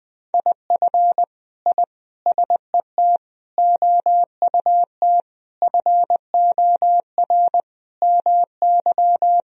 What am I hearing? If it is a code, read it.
Morse code, heard as a keyed tone, about 20 wpm, 700 Hz. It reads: IF I SET OUT FOR MY
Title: Sentences from Top 1000 Words - Review Part 2 - 20wpm